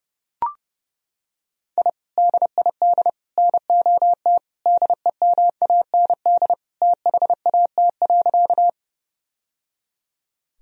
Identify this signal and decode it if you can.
Morse code, heard as a keyed tone, about 30 wpm, 700 Hz. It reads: I DID NOT DEMAND THAT.
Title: Sentences from Top 1000 Words - Review Part 4 - 30wpm